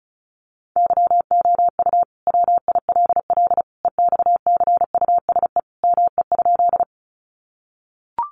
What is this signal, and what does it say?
Morse code, heard as a keyed tone, about 34 wpm, 700 Hz. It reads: YOU WILL EXCUSE ME?
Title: Sentences from Top 700 Words 35wpm